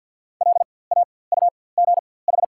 Morse code, heard as a keyed tone, about 54 wpm, spaced as for 24 wpm, 690 Hz. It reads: LAUGH
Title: 4x Character Spacing - Top 500 Words - 50wpm